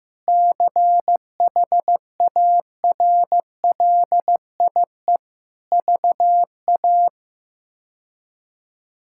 Morse code, heard as a keyed tone, about 15 wpm, 700 Hz. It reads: CHARLIE VA
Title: K1USN SST QSO Practice - 15wpm